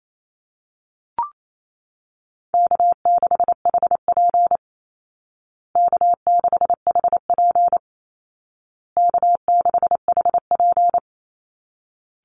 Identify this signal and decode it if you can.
Morse code, heard as a keyed tone, about 28 wpm, 700 Hz. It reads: K6HP K6HP K6HP
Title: Call Signs Repeated 2x - 28wpm